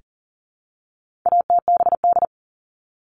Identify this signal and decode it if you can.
Morse code, heard as a keyed tone, about 40 wpm, 700 Hz. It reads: ATBD
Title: All Cut Numbers - 4-Digit 40wpm